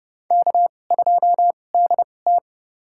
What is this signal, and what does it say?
Morse code, heard as a keyed tone, about 30 wpm, spaced as for 22 wpm, 700 Hz. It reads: K2DT
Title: USA Call Signs - 2x Character Spacing - 30wpm